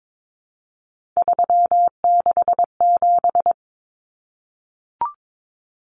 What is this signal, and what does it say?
Morse code, heard as a keyed tone, about 22 wpm, 700 Hz. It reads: 367
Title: Common Cut Numbers - 3-Digit 22wpm